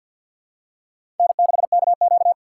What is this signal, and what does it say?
Morse code, heard as a keyed tone, about 52 wpm, 700 Hz. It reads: N6KQ
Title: USA Call Signs - 4 Characters - 3x Character Spacing - 50wpm